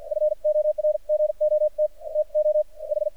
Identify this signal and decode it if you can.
Morse code, heard as a keyed tone, about 53 wpm, spaced as for 43 wpm, 610 Hz. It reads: FORGOT TO H
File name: archived / CRSnet